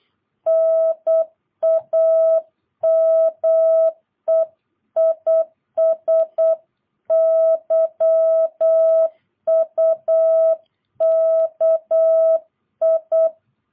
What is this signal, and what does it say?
Morse code, heard as a keyed tone, about 8 wpm, 650 Hz. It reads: NAME IS YUKI
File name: jh8cbh / 2022cw